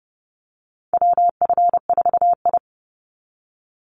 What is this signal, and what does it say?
Morse code, heard as a keyed tone, about 30 wpm, 700 Hz. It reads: WF4I
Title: NA SOTA Chaser Callsigns 30wpm